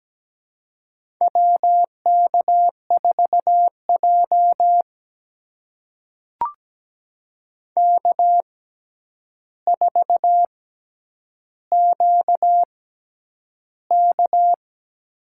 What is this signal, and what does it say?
Morse code, heard as a keyed tone, about 17 wpm, spaced as for 9 wpm, 700 Hz. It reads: WK4J K4QK
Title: USA Call Signs - 4 Characters - 6x Character Spacing - 17wpm